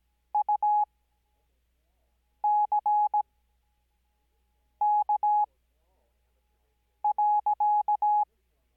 Morse code, heard as a keyed tone, about 17 wpm, spaced as for 5 wpm, 840 Hz. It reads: UCK.
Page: Farnsworth Spacing: 15WPM Character speed Spaced at 5WPM.